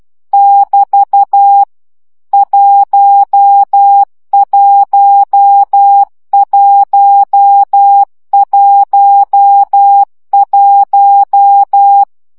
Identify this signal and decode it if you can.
Morse code, heard as a keyed tone, about 12 wpm, 800 Hz. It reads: <BT> 11111
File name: CW gyakorlas 1 10 baud